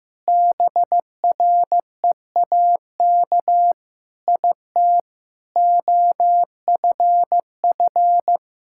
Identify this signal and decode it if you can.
Morse code, heard as a keyed tone, about 16 wpm, 700 Hz. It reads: BREAK IT OFF
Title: Sentences from Top 1000 Words - Review Part 6 - 15wpm